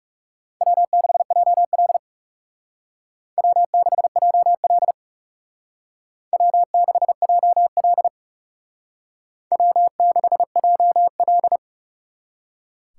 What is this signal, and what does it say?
Morse code, heard as a keyed tone, about 35 wpm, 700 Hz. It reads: W6JL W6JL W6JL W6JL
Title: Speed Racing - USA 1x2 Call Signs 30wpm